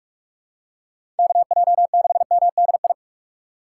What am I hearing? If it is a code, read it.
Morse code, heard as a keyed tone, about 42 wpm, 700 Hz. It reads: KJ6MDI
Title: USA Call Signs - 6 Characters - 6x Character Spacing - 45wpm